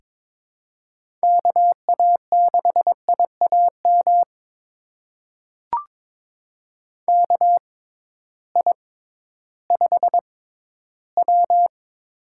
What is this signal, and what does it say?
Morse code, heard as a keyed tone, about 22 wpm, spaced as for 13 wpm, 700 Hz. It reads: KA6IAM KI5W
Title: USA Call Signs - 6x Character Spacing - 22wpm